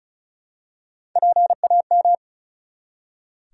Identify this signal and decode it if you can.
Morse code, heard as a keyed tone, about 34 wpm, 700 Hz. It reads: PAM
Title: USA Common Names - Max 5 Letters - 35wpm